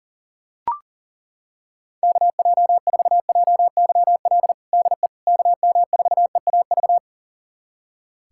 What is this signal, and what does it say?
Morse code, heard as a keyed tone, about 37 wpm, 700 Hz. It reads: KJ4JYL DE KM4EAV
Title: USA Callsign DE Callsign 40wpm